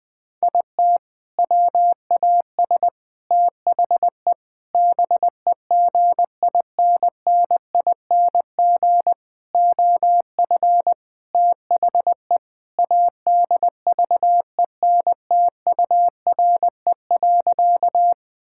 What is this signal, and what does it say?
Morse code, heard as a keyed tone, about 20 wpm, 700 Hz. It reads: IT WAS THE BEGINNING OF THE ADVENTURE.
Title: Sentences with 4-Word Phrases - 20wpm